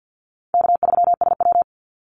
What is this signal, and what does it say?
Morse code, heard as a keyed tone, about 49 wpm, 700 Hz. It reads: K3SW
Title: Easy Call Signs Repeated 3x - 50wpm